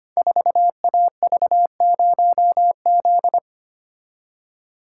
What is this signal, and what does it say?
Morse code, heard as a keyed tone, about 25 wpm, 700 Hz. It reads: 4AV0Z
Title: Code Groups - 5 Letter:Numbers - 25wpm